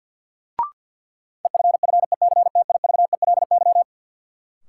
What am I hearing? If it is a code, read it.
Morse code, heard as a keyed tone, about 46 wpm, 700 Hz. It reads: EFFECTIVELY
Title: Sentences from Top 1400 Words - New Word Drill Alt - 50wpm